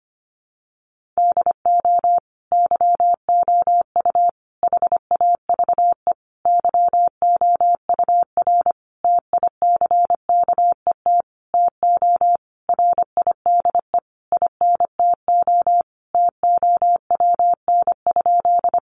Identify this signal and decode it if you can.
Morse code, heard as a keyed tone, about 25 wpm, 700 Hz. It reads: DO YOU HAVE YOUR TICKET TO RIDE INTO TOWN?
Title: Sentences from Top 1400 Words 25wpm